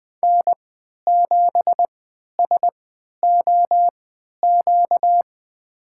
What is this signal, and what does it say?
Morse code, heard as a keyed tone, about 20 wpm, spaced as for 11 wpm, 700 Hz. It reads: N7SOQ
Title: USA Call Signs - 5 Characters - 3x Character Spacing - 20wpm